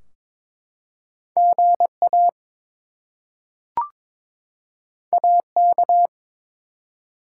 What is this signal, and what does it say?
Morse code, heard as a keyed tone, about 22 wpm, 700 Hz. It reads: GA AK
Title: US States and Canadian Provinces 22wpm